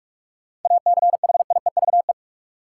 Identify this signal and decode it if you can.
Morse code, heard as a keyed tone, about 45 wpm, 700 Hz. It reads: ACHIEVE
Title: Sentences from Top 1000 Words - Review Part 5 - New Word Drill - 45wpm